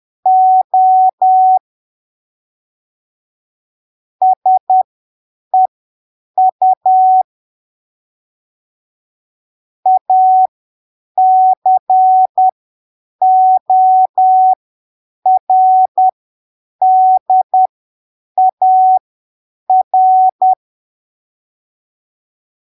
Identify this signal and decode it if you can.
Morse code, heard as a keyed tone, about 10 wpm, spaced as for 7 wpm, 750 Hz. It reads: O SEU ACORDAR
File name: arquivos / texto / texto 29 10ppm